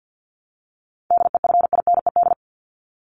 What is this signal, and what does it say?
Morse code, heard as a keyed tone, about 50 wpm, 700 Hz. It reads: DEFINED
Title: ED Suffix - Top 100 Words with Base Words 50wpm